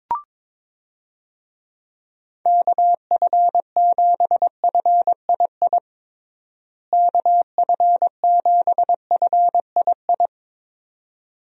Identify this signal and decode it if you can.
Morse code, heard as a keyed tone, about 21 wpm, 700 Hz. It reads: KF7FII KF7FII
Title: USA Call Signs - 6 Characters - Repeated 2x - 22wpm